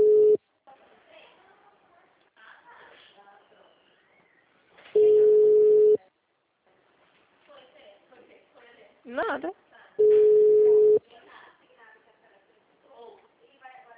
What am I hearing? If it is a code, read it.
Ringback tone (Europe, 425 Hz), with speech.
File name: out- -2006- - - .50452